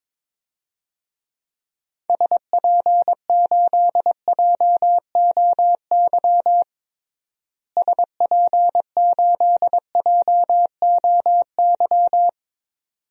Morse code, heard as a keyed tone, about 21 wpm, 700 Hz. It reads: SP8JOY SP8JOY
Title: DXCC-100 Call Signs - 6 Characters - Repeated 2x - 22wpm